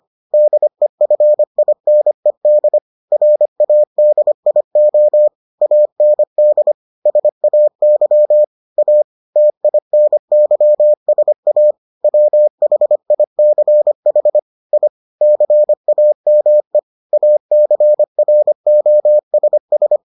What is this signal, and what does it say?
Morse code, heard as a keyed tone, about 25 wpm, 600 Hz. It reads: DEFINED RADIO AND SAY A TINYSA WHICH I CAME ACROSS